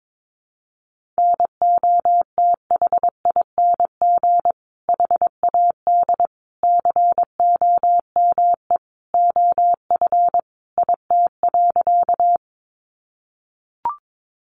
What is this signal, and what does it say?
Morse code, heard as a keyed tone, about 22 wpm, 700 Hz. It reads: NOTHING HAD COME OF IT.
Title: Sentences from Top 500 Words 22wpm